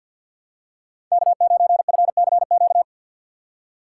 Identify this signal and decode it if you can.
Morse code, heard as a keyed tone, about 52 wpm, 700 Hz. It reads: K9FCQ
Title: USA Call Signs - 5 Characters - 3x Character Spacing - 50wpm